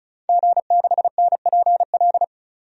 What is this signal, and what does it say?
Morse code, heard as a keyed tone, about 36 wpm, 700 Hz. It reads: G6NPL
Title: DXCC-100 Call Signs - 5 Characters - 35wpm